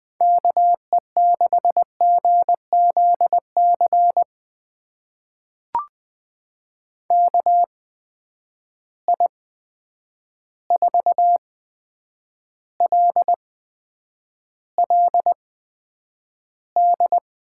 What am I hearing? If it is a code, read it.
Morse code, heard as a keyed tone, about 20 wpm, spaced as for 9 wpm, 700 Hz. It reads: KE6GZC KI4LLD
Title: USA Call Signs - 6 Characters - 8x Character Spacing - 20wpm